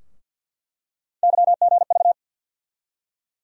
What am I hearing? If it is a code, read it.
Morse code, heard as a keyed tone, about 53 wpm, 700 Hz. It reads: YGV
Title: USA Call Sign Postfixes - 3 Letters 50wpm